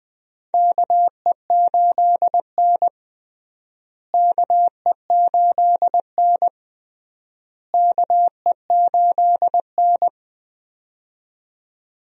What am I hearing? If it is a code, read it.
Morse code, heard as a keyed tone, about 20 wpm, 700 Hz. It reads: KE8N KE8N KE8N
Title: USA Call Signs - 4 Characters - Repeated 3x - 20wpm